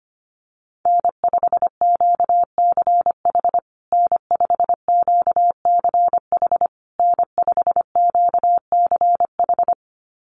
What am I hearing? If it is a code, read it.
Morse code, heard as a keyed tone, about 25 wpm, 700 Hz. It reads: N5QCH N5QCH N5QCH
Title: USA Call Signs Repeated 3x 25wpm